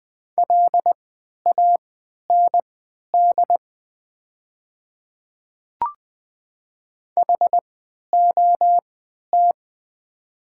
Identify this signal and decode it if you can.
Morse code, heard as a keyed tone, about 20 wpm, spaced as for 11 wpm, 700 Hz. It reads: LAND HOT
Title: 3x Character Spacing - Top 200 Words - 20wpm